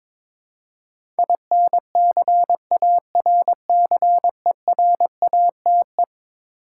Morse code, heard as a keyed tone, about 22 wpm, 700 Hz. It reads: INCARCERATE
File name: practice / 22wpm / Eleven Letter Words - 22wpm